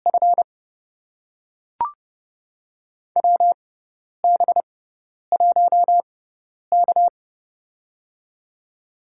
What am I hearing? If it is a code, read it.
Morse code, heard as a keyed tone, about 30 wpm, spaced as for 10 wpm, 700 Hz. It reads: F WB1K
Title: USA Call Signs - 6x Character Spacing - 30wpm